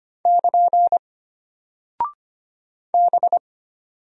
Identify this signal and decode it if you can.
Morse code, heard as a keyed tone, about 25 wpm, 700 Hz. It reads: <KN> B
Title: Single Character (Letter:Number:Punctuation:Prosign) Alt 25wpm